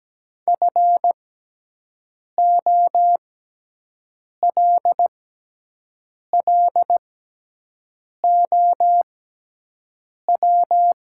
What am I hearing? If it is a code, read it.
Morse code, heard as a keyed tone, about 17 wpm, spaced as for 6 wpm, 700 Hz. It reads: FOLLOW
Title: 6x Character Spacing - Top 500 Words - 17wpm